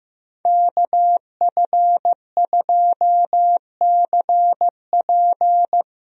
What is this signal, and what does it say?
Morse code, heard as a keyed tone, about 15 wpm, 700 Hz. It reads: KF2CP
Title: USA Call Signs - 5 Characters - Repeated 3x - 15wpm